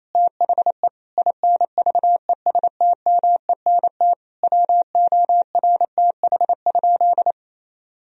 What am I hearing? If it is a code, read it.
Morse code, heard as a keyed tone, about 28 wpm, 700 Hz. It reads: THE INVESTMENT WORTH?
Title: Sentences from Top 1000 Words - Review Part 4 - 28wpm